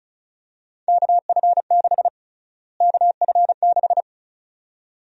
Morse code, heard as a keyed tone, about 34 wpm, 700 Hz. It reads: KF6 KF6
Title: USA Call Sign Prefixes - Repeated 2x - 35wpm